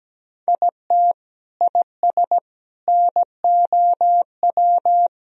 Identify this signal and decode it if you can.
Morse code, heard as a keyed tone, about 17 wpm, 700 Hz. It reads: IT IS NOW
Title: Sentences from Top 1000 Words - Review Part 2 - 17wpm